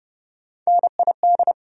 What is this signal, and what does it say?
Morse code, heard as a keyed tone, about 30 wpm, 700 Hz. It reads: NID
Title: USA Call Sign Postfixes - 3 Letters 30wpm